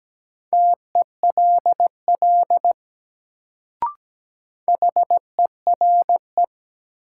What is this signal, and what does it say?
Morse code, heard as a keyed tone, about 17 wpm, 700 Hz. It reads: TELL HERE
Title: Sentences from Top 200 Words - New Word Drill - 17wpm